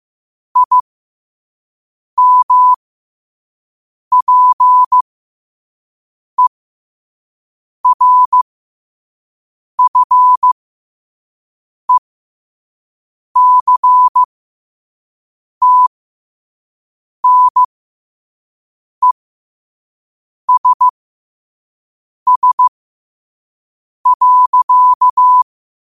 Morse code, heard as a keyed tone, about 15 wpm, spaced as for 5 wpm, 1000 Hz. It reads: IMPERFECTNESS.